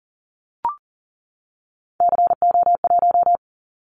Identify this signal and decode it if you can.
Morse code, heard as a keyed tone, about 40 wpm, 700 Hz. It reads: CO1
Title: Callsign Prefixes for Top 100 DXCC - 40wpm